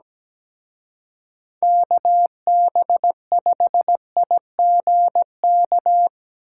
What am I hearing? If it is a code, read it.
Morse code, heard as a keyed tone, about 18 wpm, 700 Hz. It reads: KB5IGK
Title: Speed Racing - USA 2x3 Call Signs 17wpm